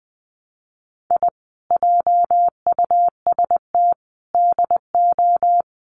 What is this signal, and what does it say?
Morse code, heard as a keyed tone, about 20 wpm, 700 Hz. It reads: I JUST DO
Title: Sets of 5 Words Encore 20wpm